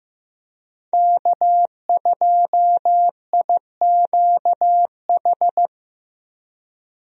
Morse code, heard as a keyed tone, about 16 wpm, 700 Hz. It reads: K2IQH
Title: USA Call Signs - 5 Characters - Repeated 3x - 15wpm